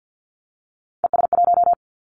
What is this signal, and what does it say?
Morse code, heard as a keyed tone, about 50 wpm, 700 Hz. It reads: ES1
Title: Three Challenging Letter-Number Combinations 50wpm